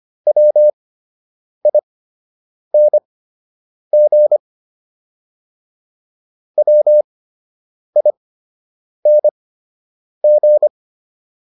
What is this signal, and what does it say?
Morse code, heard as a keyed tone, about 25 wpm, spaced as for 8 wpm, 600 Hz. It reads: WING WING